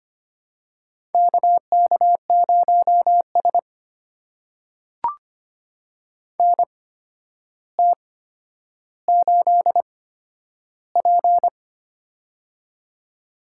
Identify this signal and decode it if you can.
Morse code, heard as a keyed tone, about 25 wpm, spaced as for 11 wpm, 700 Hz. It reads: KK0S NT8P
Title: USA Call Signs - 4 Characters - 8x Character Spacing - 25wpm